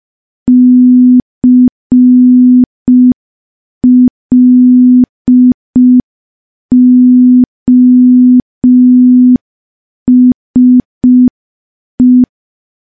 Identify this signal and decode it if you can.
Morse code, heard as a keyed tone, about 5 wpm, 250 Hz. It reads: CLOSE
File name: morse / morse-training-05WPM